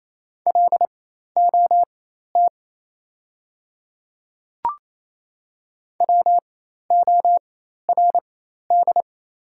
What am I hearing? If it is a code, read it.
Morse code, heard as a keyed tone, about 28 wpm, spaced as for 13 wpm, 690 Hz. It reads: LOT WORD
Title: 4x Character Spacing - Top 500 Words - 28wpm